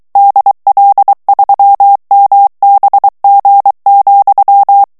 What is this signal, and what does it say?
Morse code, heard as a keyed tone, about 23 wpm, 800 Hz. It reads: DL3MBG,
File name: morse mp3 / ALLEMAND texte clair 1 vitesse 22 mots minute